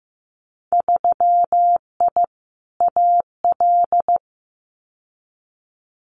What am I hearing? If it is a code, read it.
Morse code, heard as a keyed tone, about 15 wpm, 700 Hz. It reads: 3I AL
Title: Winter Field Day Exchanges Encore 15wpm